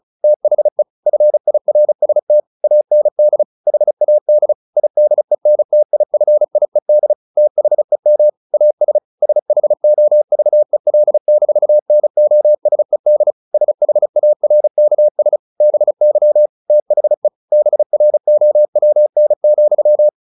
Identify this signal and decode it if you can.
Morse code, heard as a keyed tone, about 35 wpm, 600 Hz. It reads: THE FIRST AND HAD IDENTIFIED THEM AS SHOVEL-NOSED SHARKS BY THE BROWN,
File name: the old man and the sea 2 35wpm